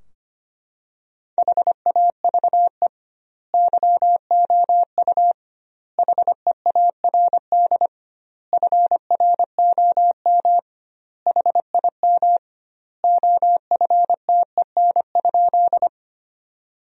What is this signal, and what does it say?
Morse code, heard as a keyed tone, about 25 wpm, 700 Hz. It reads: HAVE YOU HEARD FROM HIM OFTEN?
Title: 2x Word-Spacing - Sentences from Top 400 Words - 25wpm